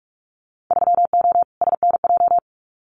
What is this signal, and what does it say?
Morse code, heard as a keyed tone, about 44 wpm, 700 Hz. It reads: 3O SNJ
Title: Winter Field Day Exchanges Encore 45wpm